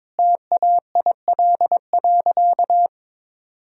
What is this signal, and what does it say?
Morse code, heard as a keyed tone, about 22 wpm, 700 Hz. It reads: TAIL.
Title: Sentences from Top 1000 Words - Review Part 3 - 22wpm